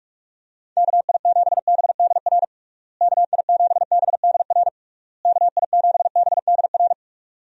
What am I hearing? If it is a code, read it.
Morse code, heard as a keyed tone, about 46 wpm, 700 Hz. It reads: KI7BDR KI7BDR KI7BDR
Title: USA Call Signs - 6 Characters - Repeated 3x - 45wpm